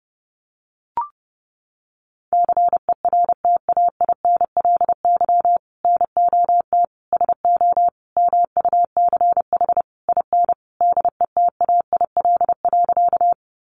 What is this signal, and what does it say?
Morse code, heard as a keyed tone, about 30 wpm, 700 Hz. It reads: CERTAINLY NOT SO MUCH IN DETAIL.
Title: Sentences from Top 1500 Words 30wpm